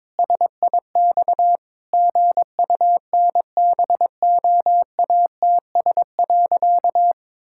Morse code, heard as a keyed tone, about 22 wpm, 700 Hz. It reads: SIX GUNBOATS.
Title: Pangram Challenge v2 - 22wpm